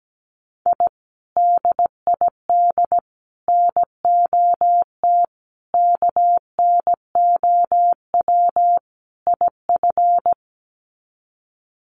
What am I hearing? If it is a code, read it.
Morse code, heard as a keyed tone, about 17 wpm, 690 Hz. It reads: I DID NOT KNOW IF
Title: Sets of 5 Words Encore 17wpm